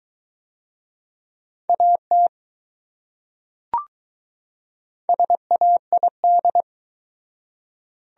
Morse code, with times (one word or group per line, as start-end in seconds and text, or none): 1.7-2.3 AT
5.1-6.6 SAID